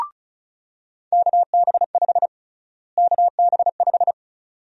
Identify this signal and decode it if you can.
Morse code, heard as a keyed tone, about 36 wpm, 700 Hz. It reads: KB5 KB5
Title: USA Call Sign Prefixes - Repeated 2x - 35wpm